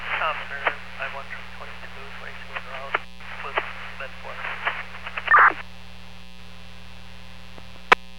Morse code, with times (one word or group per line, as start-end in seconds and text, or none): none